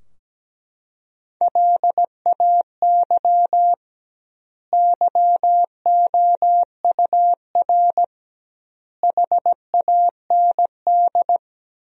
Morse code, heard as a keyed tone, about 17 wpm, 700 Hz. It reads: LAY YOUR HAND
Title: 2x Word-Spacing - Sentences from Top 400 Words - 17wpm